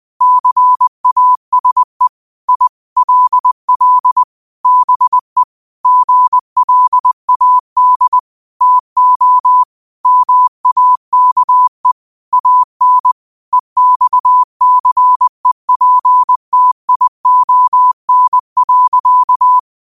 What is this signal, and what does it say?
Morse code, heard as a keyed tone, about 20 wpm, 1000 Hz. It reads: CASE I LL BE GLAD TO MAKE AN EXCEPTION.
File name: qotdfiles / 20wpm